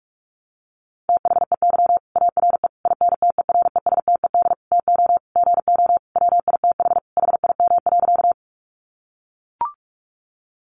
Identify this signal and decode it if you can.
Morse code, heard as a keyed tone, about 44 wpm, 700 Hz. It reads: THEY ARE INTERESTED TO GO WITH HIM.
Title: Sentences from Top 900 Words 45wpm